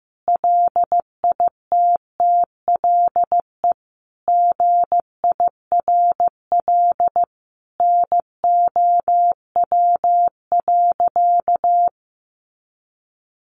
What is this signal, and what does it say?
Morse code, heard as a keyed tone, about 15 wpm, 700 Hz. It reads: LITTLE GIRL NOW.
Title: Sentences from Top 1100 Words 15wpm